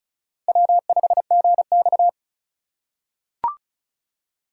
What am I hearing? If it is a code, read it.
Morse code, heard as a keyed tone, about 34 wpm, 700 Hz. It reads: W5GX
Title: USA Call Signs - 4 Characters - 2x Character Spacing - 35wpm